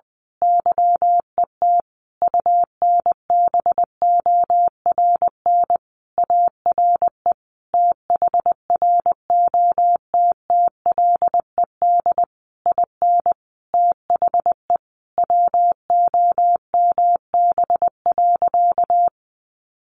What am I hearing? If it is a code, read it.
Morse code, heard as a keyed tone, about 20 wpm, 700 Hz. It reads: YET UNBORN ARE THROTTLED IN THE WOMB.
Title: A Sound Of Thunder 20wpm